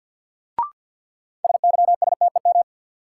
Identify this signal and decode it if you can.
Morse code, heard as a keyed tone, about 53 wpm, 700 Hz. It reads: SYSTEM
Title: Sentences from Top 1000 Words - Review Part 3 - New Word Drill - 50wpm